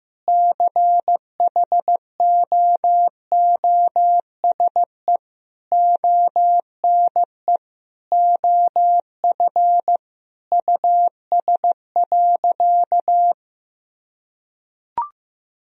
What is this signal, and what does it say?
Morse code, heard as a keyed tone, about 16 wpm, 700 Hz. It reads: CHOOSE ONE OF US.
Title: Sentences from Top 1000 Words - Review Part 4 - 15wpm